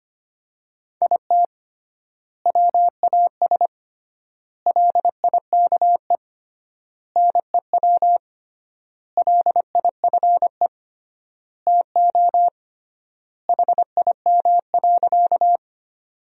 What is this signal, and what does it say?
Morse code, heard as a keyed tone, about 25 wpm, 700 Hz. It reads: IT WAS LIKE NEW LIFE TO HIM.
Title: 3x Word-Spacing - Sentences from Top 300 Words - 25wpm